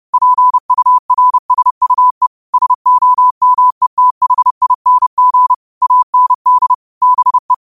Morse code, heard as a keyed tone, about 30 wpm, 1000 Hz. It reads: PURSUE SOMETHING AND BE